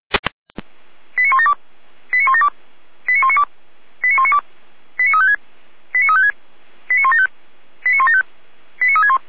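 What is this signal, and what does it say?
Five-tone selective call (ZVEI), heard as quick runs of five tones, tones of about 80 ms. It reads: 89172 89172 89192 89192 89356 89356 89175 89175 89361
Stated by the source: ZVEI five-tone selective call to address 89338, received via SDR.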